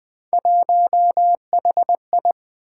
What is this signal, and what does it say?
Morse code, heard as a keyed tone, about 20 wpm, 700 Hz. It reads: 1HI
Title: Three Letter-Number Combinations - 20wpm